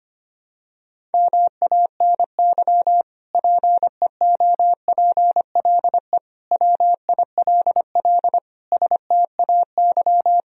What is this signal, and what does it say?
Morse code, heard as a keyed tone, about 25 wpm, 700 Hz. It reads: MANY PEOPLE WILL STAY
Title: Sentences from Top 1000 Words - Review Part 1 - 25wpm